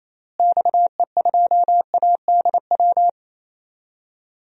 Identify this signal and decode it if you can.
Morse code, heard as a keyed tone, about 28 wpm, 700 Hz. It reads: XE2ADW
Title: DXCC-100 Call Signs - 6 Characters - Repeated 3x - 28wpm